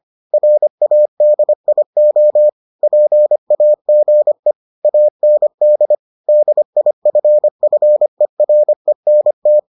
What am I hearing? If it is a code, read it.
Morse code, heard as a keyed tone, about 25 wpm, 600 Hz. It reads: RADIO PAGE AND DIFFERENT